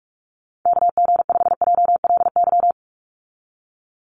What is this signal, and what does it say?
Morse code, heard as a keyed tone, about 44 wpm, 700 Hz. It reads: KG5JLY
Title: USA Call Signs 45wpm